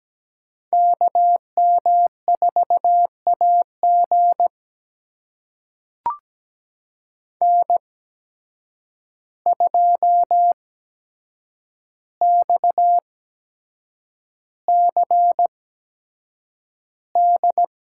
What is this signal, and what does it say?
Morse code, heard as a keyed tone, about 17 wpm, spaced as for 7 wpm, 700 Hz. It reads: KM4AG N2XCD
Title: USA Call Signs - 5 Characters - 8x Character Spacing - 17wpm